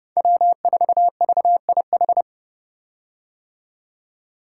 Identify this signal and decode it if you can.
Morse code, heard as a keyed tone, about 30 wpm, 700 Hz. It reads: W4VIH
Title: Code Groups - 5 Letter:Numbers - 30wpm